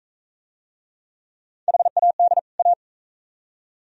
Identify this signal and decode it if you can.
Morse code, heard as a keyed tone, about 40 wpm, 690 Hz. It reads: HAD A